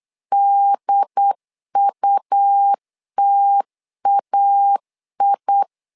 Morse code, heard as a keyed tone, about 8 wpm, 800 Hz. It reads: DUTAI
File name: LEZIONI / 3 MWRO / 3 M 8wpm 2300